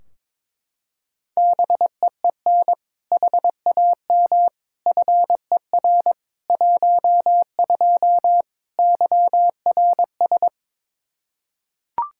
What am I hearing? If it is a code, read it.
Morse code, heard as a keyed tone, about 22 wpm, 700 Hz. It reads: BEEN HAM FER 12 YRS
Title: Practice QSO - Exchange 3 - More Info - 22wpm